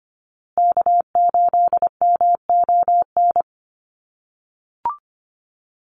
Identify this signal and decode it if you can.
Morse code, heard as a keyed tone, about 25 wpm, 700 Hz. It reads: K8MON
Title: USA Call Signs Repeated 2x 25wpm